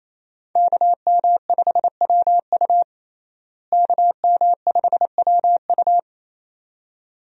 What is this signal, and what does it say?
Morse code, heard as a keyed tone, about 28 wpm, 700 Hz. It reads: KM5WU KM5WU
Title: USA Call Signs - 5 Characters - Repeated 2x - 28wpm